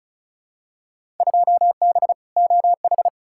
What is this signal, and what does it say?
Morse code, heard as a keyed tone, about 34 wpm, 700 Hz. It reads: 2B OH
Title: Field Day Exchanges Encore - 35wpm